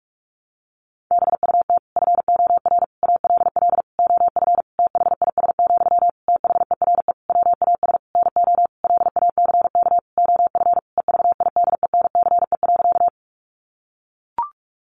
Morse code, heard as a keyed tone, about 44 wpm, 700 Hz. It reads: BUT FOR ALL OF THIS, THERE WAS NO LACK OF EVIDENCE.
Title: Sentences from Top 1300 Words 45wpm